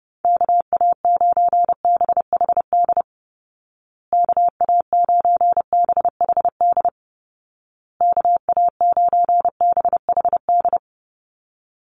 Morse code, heard as a keyed tone, about 30 wpm, 700 Hz. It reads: KA9BHD KA9BHD KA9BHD
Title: Call Signs Repeated 2x - 30wpm